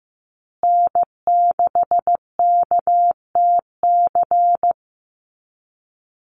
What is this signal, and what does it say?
Morse code, heard as a keyed tone, about 15 wpm, 700 Hz. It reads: N6KTC
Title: USA Call Signs Repeated 3x 15wpm